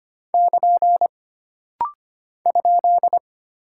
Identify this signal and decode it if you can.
Morse code, heard as a keyed tone, about 25 wpm, 700 Hz. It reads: <KN> ?
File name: Single Punctuation-Prosign - Rapid-Fire - 25wpm